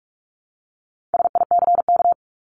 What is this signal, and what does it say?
Morse code, heard as a keyed tone, about 44 wpm, 700 Hz. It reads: SICK
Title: Top 1501-1750 Words 45wpm